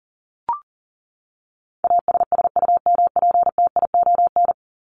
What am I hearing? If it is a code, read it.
Morse code, heard as a keyed tone, about 40 wpm, 700 Hz. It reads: ASSUMPTION
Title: Top 1751-2000 Words 40wpm